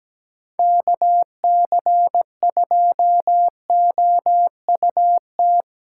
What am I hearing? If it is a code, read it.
Morse code, heard as a keyed tone, about 17 wpm, 700 Hz. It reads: KC2OUT
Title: USA Call Signs - 8x Character Spacing - 17wpm